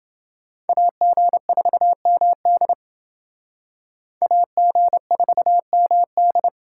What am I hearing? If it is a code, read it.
Morse code, heard as a keyed tone, about 28 wpm, 700 Hz. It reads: AG4MD AG4MD